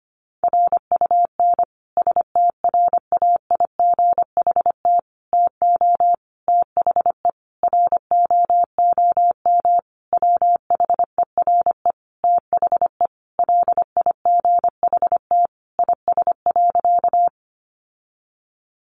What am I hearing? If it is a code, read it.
Morse code, heard as a keyed tone, about 25 wpm, 700 Hz. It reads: RUN STRAIGHT TO THE ROOM WHERE THE LIGHT IS.
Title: Sentences from Top 1000 Words 25wpm